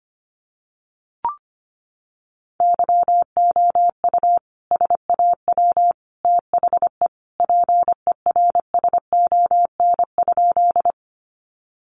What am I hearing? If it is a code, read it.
Morse code, heard as a keyed tone, about 25 wpm, 700 Hz. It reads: YOU SAW THE PERSON?
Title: Sentences from Top 400 Words 25wpm